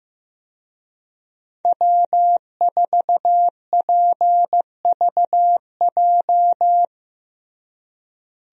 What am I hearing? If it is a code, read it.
Morse code, heard as a keyed tone, about 15 wpm, 700 Hz. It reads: W4PVJ
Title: USA Call Signs - 5 Characters - Repeated 2x - 15wpm